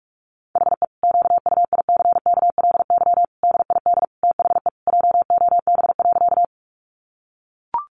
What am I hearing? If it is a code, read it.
Morse code, heard as a keyed tone, about 44 wpm, 700 Hz. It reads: HE QUICKLY DID THE JOB.
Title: Sentences from Top 700 Words 45wpm